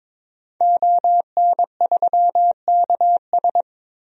Morse code, heard as a keyed tone, about 22 wpm, 700 Hz. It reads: ON3KS
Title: DXCC-100 Call Signs - 5 Characters - 22wpm